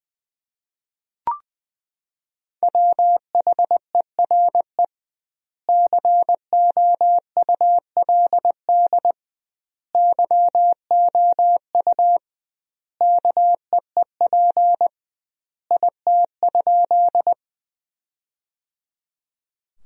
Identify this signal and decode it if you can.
Morse code, heard as a keyed tone, about 20 wpm, 700 Hz. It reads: WHERE COULD YOU KEEP IT?
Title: 2x Word-Spacing - Sentences from Top 300 Words - 20wpm